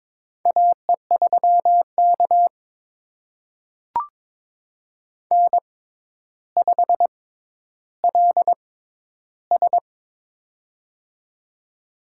Morse code, heard as a keyed tone, about 22 wpm, spaced as for 11 wpm, 700 Hz. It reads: AE3K N5LS
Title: USA Call Signs - 4 Characters - 6x Character Spacing - 22wpm